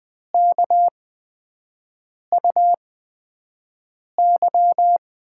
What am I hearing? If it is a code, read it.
Morse code, heard as a keyed tone, about 20 wpm, spaced as for 5 wpm, 700 Hz. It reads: KUY
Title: USA Call Signs - 6 Characters - 8x Character Spacing - 20wpm